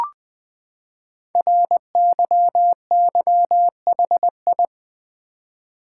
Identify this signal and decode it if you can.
Morse code, heard as a keyed tone, about 20 wpm, 700 Hz. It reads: RYYHI
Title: Code Groups - 5 Letters - 20wpm